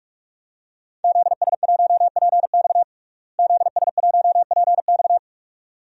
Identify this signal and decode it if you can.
Morse code, heard as a keyed tone, about 46 wpm, 700 Hz. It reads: ZS1PX ZS1PX
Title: DXCC-100 Call Signs - 5 Characters - Repeated 2x - 45wpm